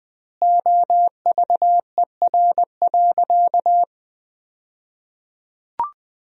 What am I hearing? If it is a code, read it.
Morse code, heard as a keyed tone, about 20 wpm, 700 Hz. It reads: OVER.
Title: 2x Word-Spacing - Sentences from Top 500 Words - 20wpm